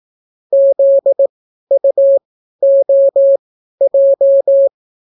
Morse code, heard as a keyed tone, about 18 wpm, spaced as for 12 wpm, 550 Hz. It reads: ZUOJ